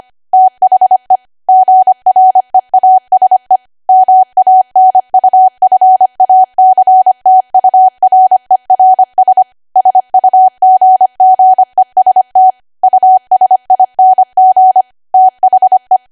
Morse code, heard as a keyed tone, about 25 wpm, 750 Hz. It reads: THE GREASE MANUFACTURERS SUGGEST USING THE
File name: Archive / 25 WPM / 25WPM